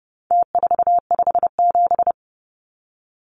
Morse code, heard as a keyed tone, about 30 wpm, 700 Hz. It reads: T457
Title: Numbers - Contest Style - 4-Digit 30wpm